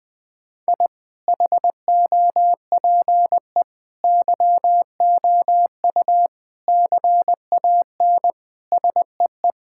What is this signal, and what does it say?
Morse code, heard as a keyed tone, about 21 wpm, 700 Hz. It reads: I HOPE YOU CAN SEE
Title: Sets of 8 Words - 20wpm